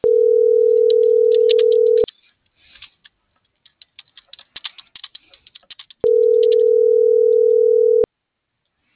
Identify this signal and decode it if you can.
Ringback tone (North America), 440 + 480 Hz.